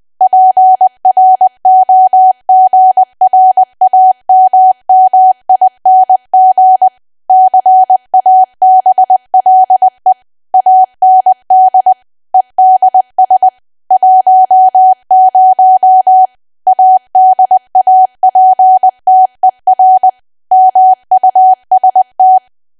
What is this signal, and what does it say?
Morse code, heard as a keyed tone, about 20 wpm, 750 Hz. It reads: PROGRAMMING CABLE AND EDS 10 ADAPTER MUST
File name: Archive / 20 WPM / 20WPM